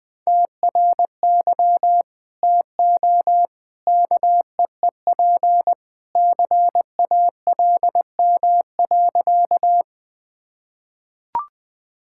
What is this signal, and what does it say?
Morse code, heard as a keyed tone, about 21 wpm, 700 Hz. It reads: TRY TO KEEP CALM.
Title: Sentences with 2-Word Phrases - 20wpm